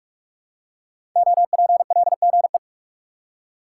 Morse code, heard as a keyed tone, about 44 wpm, 700 Hz. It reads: OPLGE
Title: Code Groups - 5 Letters - 45wpm